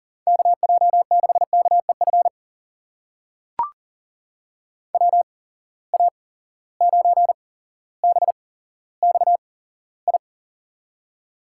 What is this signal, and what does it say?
Morse code, heard as a keyed tone, about 40 wpm, spaced as for 17 wpm, 700 Hz. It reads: KJ6KEF WA9BXI
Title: USA Call Signs - 6 Characters - 8x Character Spacing - 40wpm